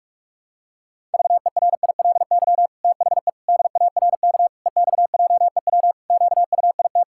Morse code, heard as a keyed tone, about 41 wpm, 700 Hz. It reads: VERILY THE DARK EXJEW QUIT
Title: Pangram Challenge v2 - 45wpm